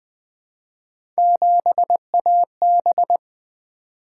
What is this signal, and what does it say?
Morse code, heard as a keyed tone, about 20 wpm, 700 Hz. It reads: 7AB
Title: Three Letter-Number Combinations - 20wpm